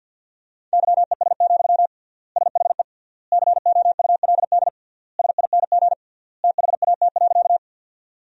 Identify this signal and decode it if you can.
Morse code, heard as a keyed tone, about 50 wpm, 700 Hz. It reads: YES, SHE COULD SING THAT.
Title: 3x Word-Spacing - Sentences from Top 500 Words - 50wpm